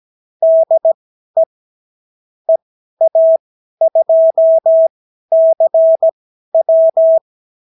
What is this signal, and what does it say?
Morse code, heard as a keyed tone, about 17 wpm, spaced as for 12 wpm, 650 Hz. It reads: DE EA2CW
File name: CW QSO libroCW4 18-8ppm